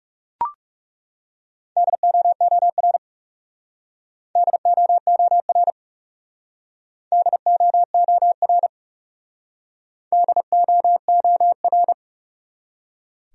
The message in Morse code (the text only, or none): DOOR DOOR DOOR DOOR